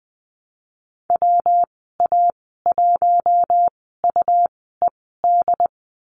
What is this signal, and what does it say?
Morse code, heard as a keyed tone, about 20 wpm, spaced as for 14 wpm, 700 Hz. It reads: WA1UED
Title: 154 Lesson 25g - Letter p - Call Signs 20wpm